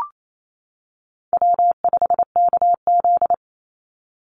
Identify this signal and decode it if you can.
Morse code, heard as a keyed tone, about 28 wpm, 700 Hz. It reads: W5KZ
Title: USA Call Signs 28wpm